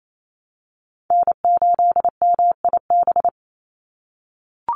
Morse code, heard as a keyed tone, about 28 wpm, 700 Hz. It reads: N8MIB
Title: USA Call Signs 28wpm